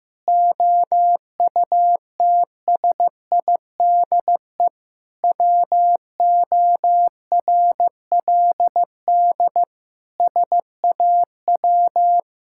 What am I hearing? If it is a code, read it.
Morse code, heard as a keyed tone, about 15 wpm, 700 Hz. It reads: OUTSIDE WORLD SAW
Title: Sentences from Top 1000 Words - Review Part 6 - 15wpm